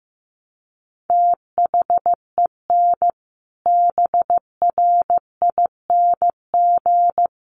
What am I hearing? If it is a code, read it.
Morse code, heard as a keyed tone, about 15 wpm, 700 Hz. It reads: THEN BRING
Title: Sentences from Top 1500 Words 15wpm